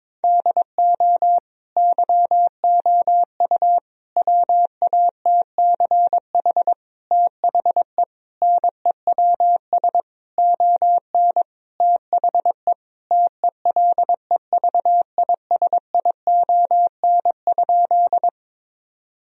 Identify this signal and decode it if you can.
Morse code, heard as a keyed tone, about 22 wpm, 700 Hz. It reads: DO YOU WATCH THE NEWS ON THE TELEVISION?
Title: Sentences from Top 1000 Words - Review Part 4 - 22wpm